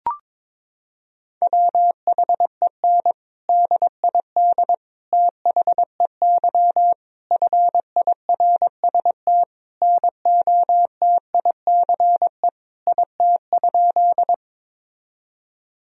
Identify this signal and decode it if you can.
Morse code, heard as a keyed tone, about 21 wpm, 700 Hz. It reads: WHEN DID THEY FIRST NOTICE IT?
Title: Sentences from Top 1000 Words - Review Part 2 - 22wpm